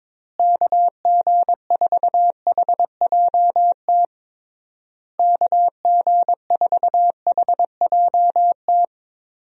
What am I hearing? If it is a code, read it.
Morse code, heard as a keyed tone, about 22 wpm, 700 Hz. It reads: KG4HJT KG4HJT
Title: USA Call Signs - 6 Characters - Repeated 3x - 22wpm